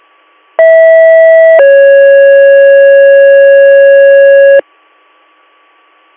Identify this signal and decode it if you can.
Two-tone page, heard as a one-second tone, then a three-second tone, about 650 then 550 Hz. It is heard from 0.6 to 4.6 s.